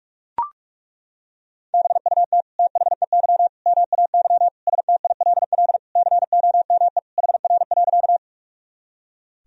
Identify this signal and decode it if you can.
Morse code, heard as a keyed tone, about 41 wpm, 700 Hz. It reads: BUT THEY MAY STILL COME HR.
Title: Sentences from Top 300 Words and Abbreviations 45wpm